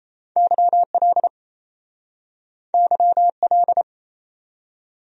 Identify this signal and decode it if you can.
Morse code, heard as a keyed tone, about 37 wpm, 700 Hz. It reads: YL YL